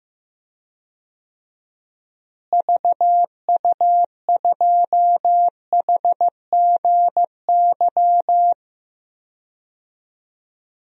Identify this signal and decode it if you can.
Morse code, heard as a keyed tone, about 15 wpm, 700 Hz. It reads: VU2HGY
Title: DXCC-100 Call Signs - 6 Characters - Repeated 3x - 15wpm